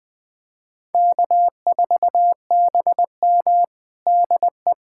Morse code, heard as a keyed tone, about 21 wpm, 700 Hz. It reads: K4BM DE
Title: USA Callsign DE Callsign - 4 Characters - 20wpm